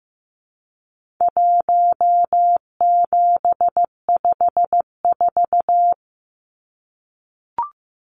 Morse code, heard as a keyed tone, about 15 wpm, 700 Hz. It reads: 1754
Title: Numbers - Contest Style - 4-Digit 15wpm